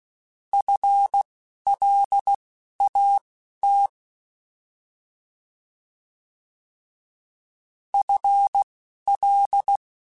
Morse code, heard as a keyed tone, about 16 wpm, spaced as for 12 wpm, 800 Hz. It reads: FLAT FL
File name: FL / FL mp3s / Fast FL @10